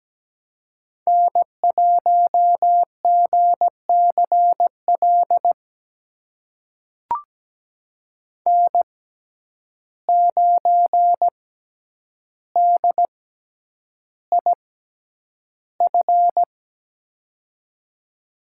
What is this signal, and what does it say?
Morse code, heard as a keyed tone, about 17 wpm, spaced as for 9 wpm, 700 Hz. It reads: N1GCL N9DIF
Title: USA Call Signs - 5 Characters - 6x Character Spacing - 17wpm